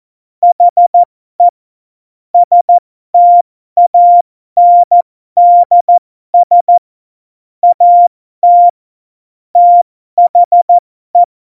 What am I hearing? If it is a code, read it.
Morse code, heard as a keyed tone, about 14 wpm, spaced as for 12 wpm, 700 Hz. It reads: HE STANDS AT THE
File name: Morsekurs / Witze